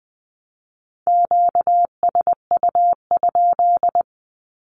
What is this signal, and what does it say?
Morse code, heard as a keyed tone, about 20 wpm, 700 Hz. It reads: QSU?
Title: Q-Codes 20wpm